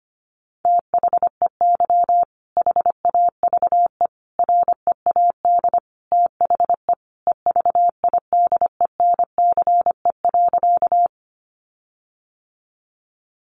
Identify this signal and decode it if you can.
Morse code, heard as a keyed tone, about 25 wpm, 700 Hz. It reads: THEY HAVE READ THE EVIDENCE.
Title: Sentences from Top 1300 Words 25wpm